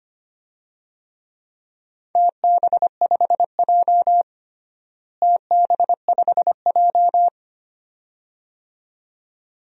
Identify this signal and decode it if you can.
Morse code, heard as a keyed tone, about 25 wpm, 700 Hz. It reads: TB5J TB5J
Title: DXCC-100 Call Signs - 4 Characters - Repeated 2x - 25wpm